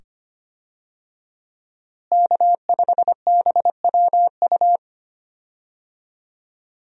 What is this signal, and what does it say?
Morse code, heard as a keyed tone, about 25 wpm, 700 Hz. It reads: K5BWU
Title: USA Call Signs - 5 Characters - Repeated 2x - 25wpm